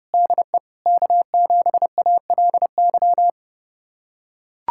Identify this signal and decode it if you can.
Morse code, heard as a keyed tone, about 32 wpm, 700 Hz. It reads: DE K7ALY
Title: USA Callsign DE Callsign - 5 Characters - 30wpm